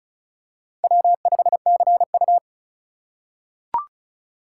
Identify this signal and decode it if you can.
Morse code, heard as a keyed tone, about 36 wpm, 700 Hz. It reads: W5CU
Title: USA Call Signs - 4x Character Spacing - 35wpm